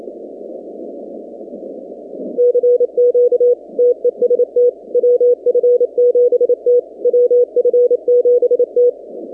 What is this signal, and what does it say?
Morse code, heard as a keyed tone, about 27 wpm, 490 Hz. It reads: CQ TEST WF7T WF7T
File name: 2019-cw-wpx-kt5j-k5pi